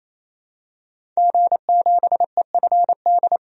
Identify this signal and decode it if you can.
Morse code, heard as a keyed tone, about 28 wpm, 700 Hz. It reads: G7EFD
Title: DXCC-100 Call Signs - 5 Characters - 28wpm